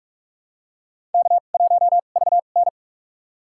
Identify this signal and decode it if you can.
Morse code, heard as a keyed tone, about 42 wpm, spaced as for 31 wpm, 700 Hz. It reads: K1VN
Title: USA Call Signs - 2x Character Spacing - 45wpm